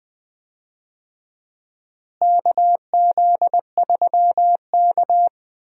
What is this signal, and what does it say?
Morse code, heard as a keyed tone, about 20 wpm, 700 Hz. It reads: KZ3K
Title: USA Call Signs - 4 Characters - Repeated 3x - 20wpm